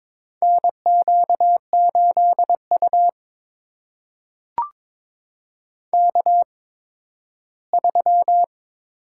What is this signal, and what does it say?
Morse code, heard as a keyed tone, about 21 wpm, spaced as for 13 wpm, 700 Hz. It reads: NQ8U K3
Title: USA Call Signs - 4 Characters - 8x Character Spacing - 22wpm